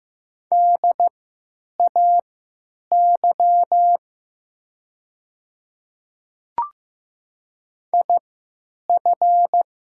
Morse code, heard as a keyed tone, about 15 wpm, spaced as for 9 wpm, 690 Hz. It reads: DAY IF
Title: 3x Character Spacing - Top 300 Words - 15wpm